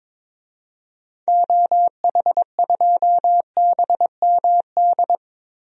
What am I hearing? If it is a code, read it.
Morse code, heard as a keyed tone, about 22 wpm, 700 Hz. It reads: OH2BMD
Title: DXCC-100 Call Signs - 6 Characters - Repeated 3x - 22wpm